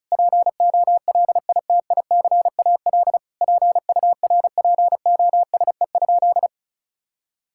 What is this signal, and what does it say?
Morse code, heard as a keyed tone, about 34 wpm, 700 Hz. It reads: POLITICAL PURPOSE?
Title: Sentences from Top 1000 Words - Review Part 5 - 35wpm